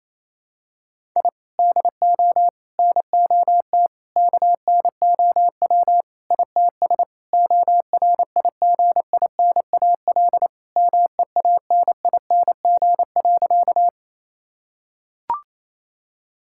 Morse code, heard as a keyed tone, about 28 wpm, 700 Hz. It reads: I DO NOT KNOW ITS ORIGINAL MEANING.
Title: Sentences from Top 1000 Words - Review Part 6 - 28wpm